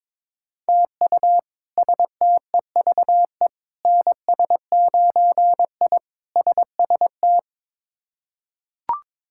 Morse code, heard as a keyed tone, about 22 wpm, 700 Hz. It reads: TU STEVE NS9I SST
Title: K1USN SST QSO Practice - 22wpm